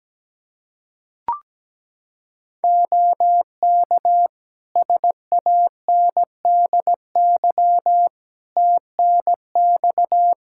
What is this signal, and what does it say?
Morse code, heard as a keyed tone, about 17 wpm, 700 Hz. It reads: OK SANDY TNX
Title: K1USN SST QSO Practice 17wpm